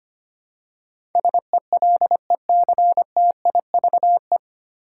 Morse code, heard as a keyed tone, about 25 wpm, 700 Hz. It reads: SELECTIVE
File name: Nine Letter Words - 25wpm